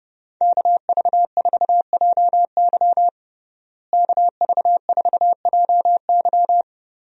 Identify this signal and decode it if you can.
Morse code, heard as a keyed tone, about 31 wpm, 700 Hz. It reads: KV4JY KV4JY
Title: USA Call Signs - 5 Characters - Repeated 3x - 30wpm